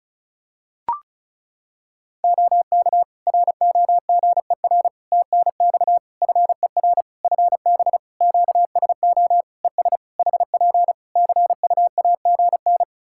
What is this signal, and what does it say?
Morse code, heard as a keyed tone, about 36 wpm, 700 Hz. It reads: OK ROGER TNX FER FB QSO ES HP CUAGN
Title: Practice QSO - Exchange 4 - Ending - 35wpm